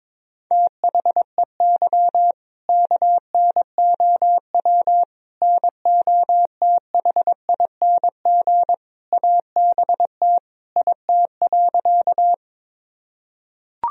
Morse code, heard as a keyed tone, about 22 wpm, 700 Hz. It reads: THEY KNOW NOTHING ABT IT.
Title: Sentences from Top 500 Words and Abbreviations 22wpm